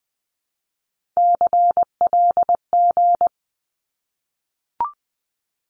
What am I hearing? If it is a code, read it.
Morse code, heard as a keyed tone, about 20 wpm, 700 Hz. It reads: CLG
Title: Abbreviations Repeated 3x - 20wpm